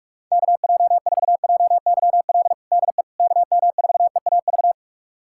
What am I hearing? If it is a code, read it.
Morse code, heard as a keyed tone, about 46 wpm, 690 Hz. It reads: KJ4JYL DE KM4EAV
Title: USA Callsign DE Callsign 45wpm